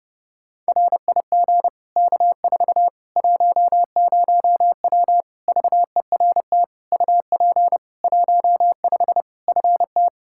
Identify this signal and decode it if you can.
Morse code, heard as a keyed tone, about 30 wpm, 700 Hz. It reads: RIG K4 10W VERT UP 15 FT
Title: Practice QSO - Exchange 2 - Info - 30wpm